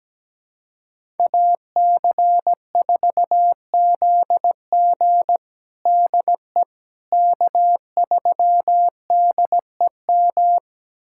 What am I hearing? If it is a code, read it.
Morse code, heard as a keyed tone, about 17 wpm, 700 Hz. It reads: AC4ZG DE K3DEM
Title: USA Callsign DE Callsign - 5 Characters - 17wpm